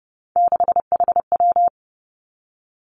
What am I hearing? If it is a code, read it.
Morse code, heard as a keyed tone, about 30 wpm, 700 Hz. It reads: 6HW
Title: Three Challenging Letter-Number Combinations 30wpm